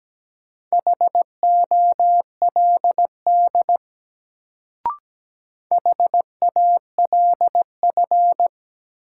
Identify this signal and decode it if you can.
Morse code, heard as a keyed tone, about 17 wpm, 700 Hz. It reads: HOLD HALF
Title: Sentences from Top 400 Words - New Word Drill - 17wpm